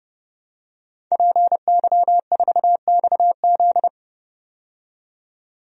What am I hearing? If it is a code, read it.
Morse code, heard as a keyed tone, about 32 wpm, 700 Hz. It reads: PY4XZ
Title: DXCC-100 Call Signs - 5 Characters - 30wpm